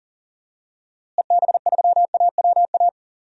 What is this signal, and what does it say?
Morse code, heard as a keyed tone, about 40 wpm, 700 Hz. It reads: EB3AWA
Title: DXCC-100 Call Signs - 6 Characters - Repeated 2x - 40wpm